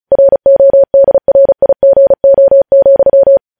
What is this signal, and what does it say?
Morse code, heard as a keyed tone, about 36 wpm, 550 Hz. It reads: RODRIGO,
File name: podcast / News headlines-1x-35x35